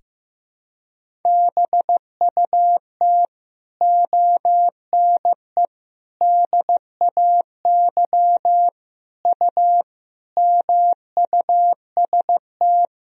Morse code, heard as a keyed tone, about 15 wpm, 700 Hz. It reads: BUT ONE DAY U MUST
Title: Sentences from Top 400 Words and Abbreviations 15wpm